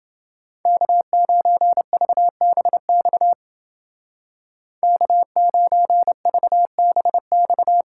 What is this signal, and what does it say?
Morse code, heard as a keyed tone, about 28 wpm, 700 Hz. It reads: K9VBX K9VBX